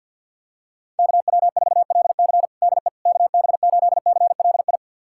Morse code, heard as a keyed tone, about 49 wpm, 700 Hz. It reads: KW4LC DE KB8CLI
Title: USA Callsign DE Callsign 50wpm